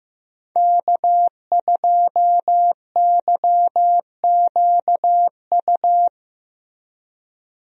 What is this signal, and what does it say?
Morse code, heard as a keyed tone, about 15 wpm, 700 Hz. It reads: K2YQU
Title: USA Call Signs - 5 Characters - Repeated 3x - 15wpm